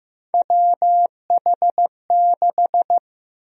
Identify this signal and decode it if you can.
Morse code, heard as a keyed tone, about 15 wpm, 700 Hz. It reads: WH6
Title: USA Call Sign Prefixes - Repeated 2x - 15wpm